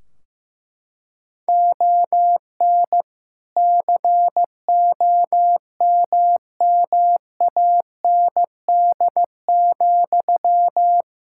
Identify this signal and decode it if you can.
Morse code, heard as a keyed tone, about 16 wpm, 700 Hz. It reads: ON COMMAND,
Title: Sentences from Top 1600 Words 15wpm